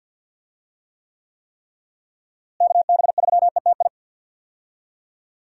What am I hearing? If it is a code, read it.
Morse code, heard as a keyed tone, about 49 wpm, 700 Hz. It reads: KB3ETI
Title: USA Call Signs - 6 Characters - Repeated 3x - 50wpm